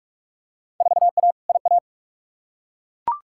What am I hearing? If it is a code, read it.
Morse code, heard as a keyed tone, about 41 wpm, 700 Hz. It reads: 4A IA
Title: Field Day Exchanges - 45wpm